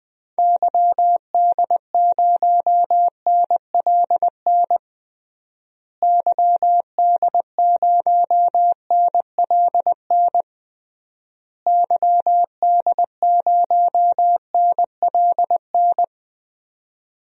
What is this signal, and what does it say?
Morse code, heard as a keyed tone, about 21 wpm, 700 Hz. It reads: YD0NLN YD0NLN YD0NLN
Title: DXCC-100 Call Signs - 6 Characters - Repeated 3x - 20wpm